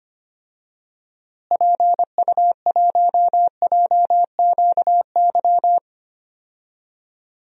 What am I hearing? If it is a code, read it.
Morse code, heard as a keyed tone, about 25 wpm, 700 Hz. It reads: PU1JQY
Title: DXCC-100 Call Signs - 6 Characters - Repeated 3x - 25wpm